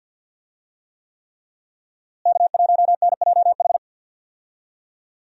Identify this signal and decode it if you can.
Morse code, heard as a keyed tone, about 49 wpm, 700 Hz. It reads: K1NJH
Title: USA Call Signs - 5 Characters - Repeated 3x - 50wpm